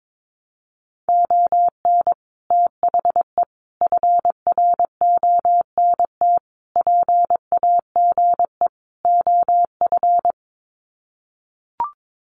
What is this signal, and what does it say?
Morse code, heard as a keyed tone, about 22 wpm, 700 Hz. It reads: ON THE FRONT PAGE OF
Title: Sets of 5 Words Encore 22wpm